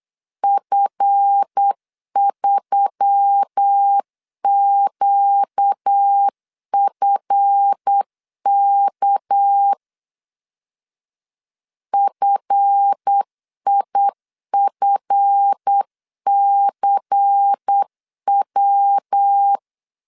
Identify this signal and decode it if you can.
Morse code, heard as a keyed tone, about 8 wpm, 800 Hz. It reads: F3QFK FIFCW
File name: LEZIONI / 7 GF9Z / 7 GF 8wpm 3108